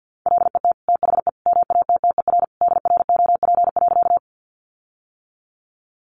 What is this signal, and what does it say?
Morse code, heard as a keyed tone, about 50 wpm, 700 Hz. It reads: LET THE MATTER DROP.
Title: Sentences from Top 1400 Words 50wpm